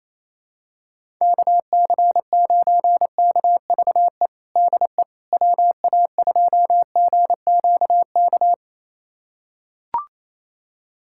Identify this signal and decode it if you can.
Morse code, heard as a keyed tone, about 28 wpm, 700 Hz. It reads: KC9KVE DE WA2GQK
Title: USA Callsign DE Callsign - 6 Characters - 28wpm